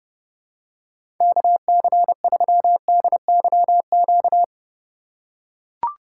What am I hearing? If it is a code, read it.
Morse code, heard as a keyed tone, about 30 wpm, 700 Hz. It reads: KC3DYQ
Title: USA Call Signs - 4x Character Spacing - 30wpm